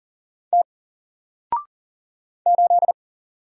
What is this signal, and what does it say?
Morse code, heard as a keyed tone, about 40 wpm, 700 Hz. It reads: T 8
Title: Single Letter-Number Alt 40wpm